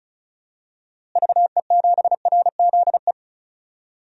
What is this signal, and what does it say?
Morse code, heard as a keyed tone, about 34 wpm, 700 Hz. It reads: VE7RZE
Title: DXCC-100 Call Signs - 6 Characters - Repeated 3x - 35wpm